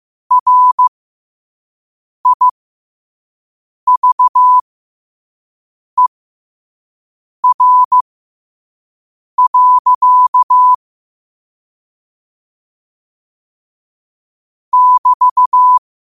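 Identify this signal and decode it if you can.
Morse code, heard as a keyed tone, about 15 wpm, spaced as for 5 wpm, 1000 Hz. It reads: RIVER. <BT>
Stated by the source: Morse at 5 wpm.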